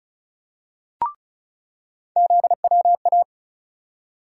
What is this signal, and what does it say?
Morse code, heard as a keyed tone, about 36 wpm, 700 Hz. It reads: ZWA
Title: USA Call Sign Postfixes - 3 Letters 35wpm